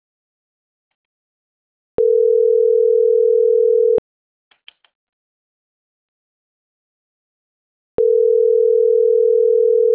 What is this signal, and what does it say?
Ringback tone (North America), 440 + 480 Hz.